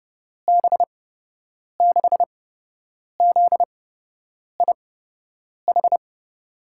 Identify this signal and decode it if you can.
Morse code, heard as a keyed tone, about 30 wpm, spaced as for 8 wpm, 700 Hz. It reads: B6ZIH
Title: USA Call Signs - 6 Characters - 8x Character Spacing - 30wpm